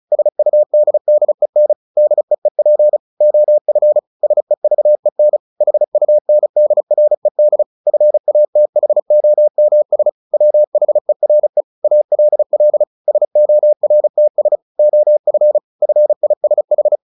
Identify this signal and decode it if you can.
Morse code, heard as a keyed tone, about 34 wpm, 600 Hz. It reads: SUDDEN DEEP OF SEVEN HUNDRED FATHOMS WHERE ALL SORTS OF FISH
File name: the old man and the sea / the old man and the sea 1 35wpm